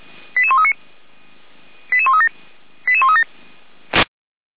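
Five-tone selective call (ZVEI), heard as quick runs of five tones, tones of about 80 ms. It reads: 88139 88137 88137
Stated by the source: ZVEI five-tone selective call received by SDR, recorded on the audio server of Feuerwehr Orlamünde.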